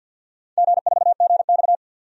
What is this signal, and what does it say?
Morse code, heard as a keyed tone, about 53 wpm, 700 Hz. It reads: G4GX
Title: DXCC-100 Call Signs - 4 Characters - 50wpm